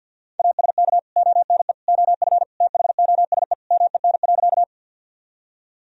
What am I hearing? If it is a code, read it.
Morse code, heard as a keyed tone, about 47 wpm, 700 Hz. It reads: ASK ONE OF THOSE MEN.
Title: Sentences from Top 1000 Words - Review Part 1 - 50wpm